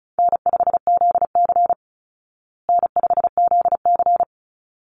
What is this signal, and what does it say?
Morse code, heard as a keyed tone, about 34 wpm, 700 Hz. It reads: N5ZC N5ZC
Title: Easy Call Signs Repeated 2x - 35wpm